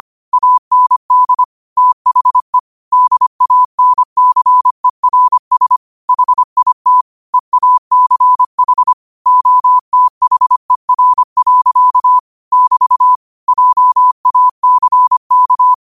Morse code, heard as a keyed tone, about 25 wpm, 1000 Hz. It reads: AND THE DANCERS HIT EACH OTHER. <BT> JACK
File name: qotdfiles / 25wpm